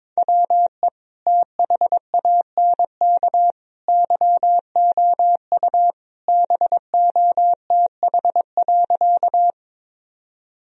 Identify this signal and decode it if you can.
Morse code, heard as a keyed tone, about 22 wpm, 700 Hz. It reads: WE THANK YOU BOTH.
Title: Sentences from Top 1000 Words - Review Part 1 - 22wpm